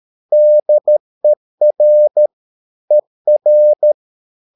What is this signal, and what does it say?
Morse code, heard as a keyed tone, about 13 wpm, 600 Hz. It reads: DER ER
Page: Ferie 13 wpm